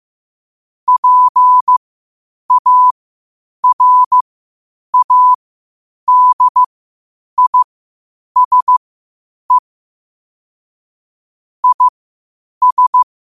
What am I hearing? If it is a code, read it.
Morse code, heard as a keyed tone, about 15 wpm, spaced as for 8 wpm, 1000 Hz. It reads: PARADISE IS
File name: qotdfiles / 8wpm